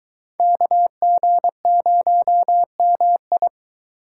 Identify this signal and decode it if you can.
Morse code, heard as a keyed tone, about 24 wpm, 700 Hz. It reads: KG0MI